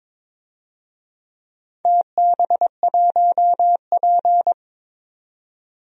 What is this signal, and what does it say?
Morse code, heard as a keyed tone, about 22 wpm, 700 Hz. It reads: TB1P
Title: DXCC-100 Call Signs - 4 Characters - 22wpm